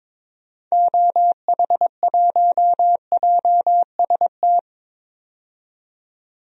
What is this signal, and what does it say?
Morse code, heard as a keyed tone, about 22 wpm, 700 Hz. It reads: OH1JST
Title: DXCC-100 Call Signs - 6 Characters - 22wpm